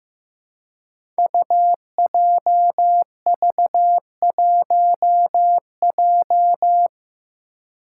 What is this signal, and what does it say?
Morse code, heard as a keyed tone, about 15 wpm, 700 Hz. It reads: UJV1J
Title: Five Challenging Letter-Number Combinations 15wpm